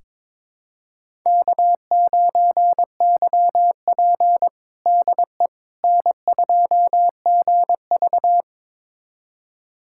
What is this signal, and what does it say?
Morse code, heard as a keyed tone, about 22 wpm, 700 Hz. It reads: K9YP DE N2GV
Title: USA Callsign DE Callsign - 4 Characters - 22wpm